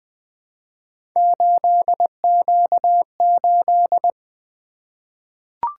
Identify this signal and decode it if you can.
Morse code, heard as a keyed tone, about 20 wpm, 700 Hz. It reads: 8Q8
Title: Three Letter-Number Combinations - 20wpm